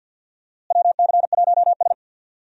Morse code, heard as a keyed tone, about 52 wpm, 700 Hz. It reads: WC1S
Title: USA Call Signs - 4 Characters - Repeated 3x - 50wpm